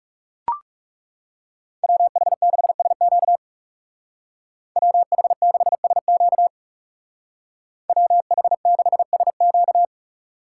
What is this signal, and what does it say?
Morse code, heard as a keyed tone, about 35 wpm, 700 Hz. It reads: WH6SQ WH6SQ WH6SQ
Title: Speed Racing - USA Call Signs 30wpm